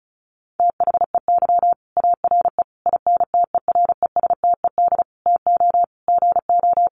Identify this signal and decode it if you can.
Morse code, heard as a keyed tone, about 34 wpm, 700 Hz. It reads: THEY ARE INTERESTED TO GO
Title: Sentences from Top 900 Words 35wpm